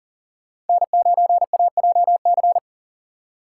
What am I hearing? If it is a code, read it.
Morse code, heard as a keyed tone, about 40 wpm, 690 Hz. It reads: N9AJC
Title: USA Call Signs - 4x Character Spacing - 40wpm